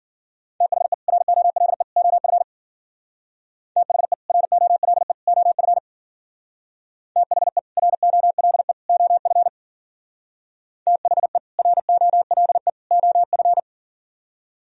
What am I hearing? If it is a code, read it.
Morse code, heard as a keyed tone, about 51 wpm, 700 Hz. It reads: THE ROLE OF THE ROLE OF THE ROLE OF THE ROLE OF